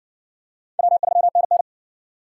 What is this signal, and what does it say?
Morse code, heard as a keyed tone, about 60 wpm, 700 Hz. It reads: W3NN